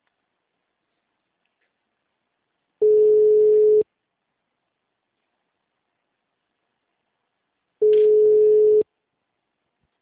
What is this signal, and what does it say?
Ringback tone (Europe, 425 Hz).